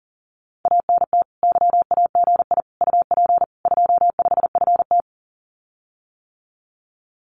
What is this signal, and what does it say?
Morse code, heard as a keyed tone, about 41 wpm, 700 Hz. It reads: ANT YAGI UP 25FT
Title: Common QSO Phrases Encore 40wpm